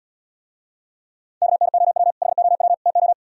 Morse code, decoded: KNOW HOW TO

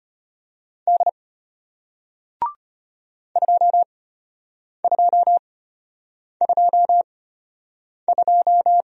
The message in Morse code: D 2222

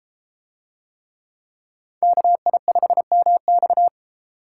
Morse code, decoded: KI5MX

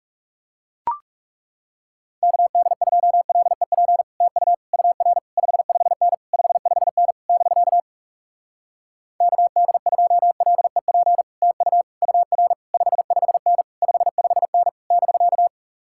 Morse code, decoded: KD2LEP TU UR 55N 55N <BK> KD2LEP TU UR 55N 55N <BK>

